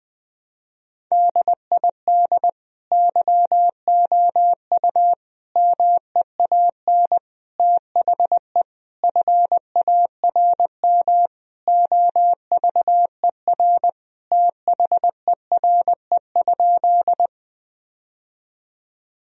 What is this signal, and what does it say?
Morse code, heard as a keyed tone, about 20 wpm, 700 Hz. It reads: DID YOU MEAN THE FARM OVER THERE?
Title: Sentences from Top 1000 Words - Review Part 2 - 20wpm